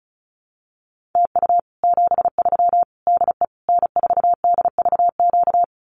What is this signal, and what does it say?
Morse code, heard as a keyed tone, about 34 wpm, 700 Hz. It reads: TU 73 DE N4DVQ
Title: Common QSO Phrases Encore 35wpm